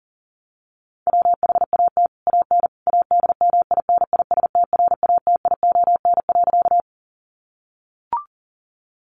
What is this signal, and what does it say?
Morse code, heard as a keyed tone, about 40 wpm, 700 Hz. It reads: WHAT AN ADMINISTRATION.
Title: Sentences from Top 1300 Words 40wpm